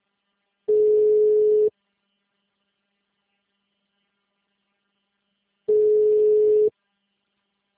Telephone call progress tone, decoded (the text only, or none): ringback tone